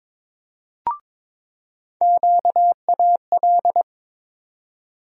Morse code, heard as a keyed tone, about 22 wpm, 700 Hz. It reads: QAL
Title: USA Call Sign Postfixes - 3 Letters 22wpm